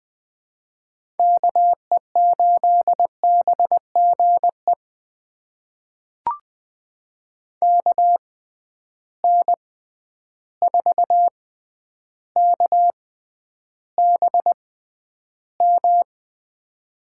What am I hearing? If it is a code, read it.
Morse code, heard as a keyed tone, about 20 wpm, spaced as for 10 wpm, 700 Hz. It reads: KE8BGE KN4KBM
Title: USA Call Signs - 6 Characters - 6x Character Spacing - 20wpm